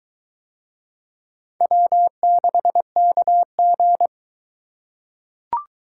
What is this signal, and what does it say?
Morse code, heard as a keyed tone, about 22 wpm, 700 Hz. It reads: W6KG